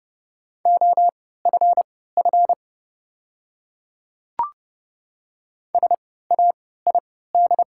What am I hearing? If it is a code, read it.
Morse code, heard as a keyed tone, about 30 wpm, spaced as for 17 wpm, 690 Hz. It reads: OFF SAID
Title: 3x Character Spacing - Top 300 Words - 30wpm